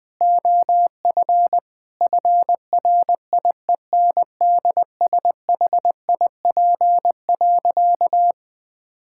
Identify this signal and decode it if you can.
Morse code, heard as a keyed tone, about 20 wpm, 700 Hz. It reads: OF FRIENDSHIP.
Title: Difficult Sentences with Hints - 20wpm